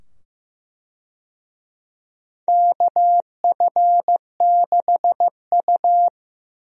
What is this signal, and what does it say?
Morse code, heard as a keyed tone, about 15 wpm, 700 Hz. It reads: KF6U
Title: USA Call Signs - 4 Characters - 15wpm